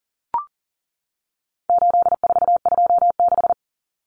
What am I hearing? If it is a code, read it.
Morse code, heard as a keyed tone, about 40 wpm, 700 Hz. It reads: 8426
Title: Numbers - 4-Digit 40wpm